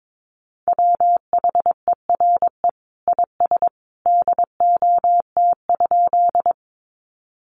Morse code, heard as a keyed tone, about 22 wpm, 700 Hz. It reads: WHERE IS DOT?
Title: Sentences from Top 1500 Words 22wpm